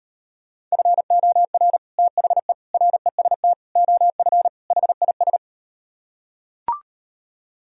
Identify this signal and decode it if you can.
Morse code, heard as a keyed tone, about 38 wpm, 700 Hz. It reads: FOR THE REST OF HIS